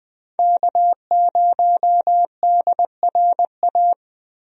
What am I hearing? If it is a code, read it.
Morse code, heard as a keyed tone, about 20 wpm, 700 Hz. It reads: K0DRA
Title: USA Call Signs - 5 Characters - Repeated 2x - 20wpm